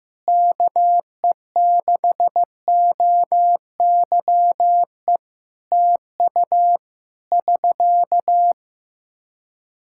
Morse code, heard as a keyed tone, about 15 wpm, 700 Hz. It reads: KE6OYE TU <SK>
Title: Practice QSO - Exchange 4 - Ending - 15wpm